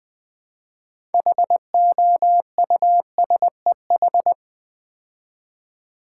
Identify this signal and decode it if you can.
Morse code, heard as a keyed tone, about 20 wpm, 700 Hz. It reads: HOUSEH
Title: Build A Word - 3 Letters At A Time - 9-Letter Words - 20wpm